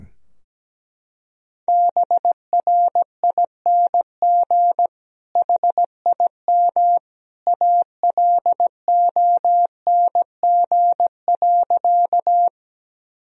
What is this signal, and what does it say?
Morse code, heard as a keyed tone, about 17 wpm, 700 Hz. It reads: BRING HIM ALONG.
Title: Sentences from Top 1000 Words - Review Part 3 - 17wpm